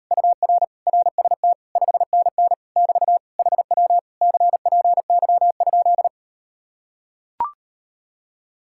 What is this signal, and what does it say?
Morse code, heard as a keyed tone, about 39 wpm, 700 Hz. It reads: UR RST 5NN <BT> HW CPY?